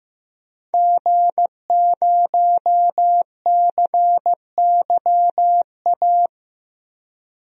Morse code, heard as a keyed tone, about 15 wpm, 700 Hz. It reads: G0CYA
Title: DXCC-100 Call Signs - 5 Characters - 15wpm